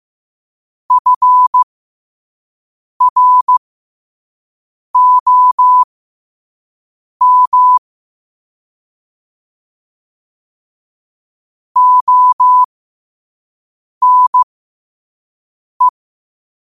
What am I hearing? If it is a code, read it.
Morse code, heard as a keyed tone, about 15 wpm, spaced as for 5 wpm, 1000 Hz. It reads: FROM ONE
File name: qotdfiles / 5wpm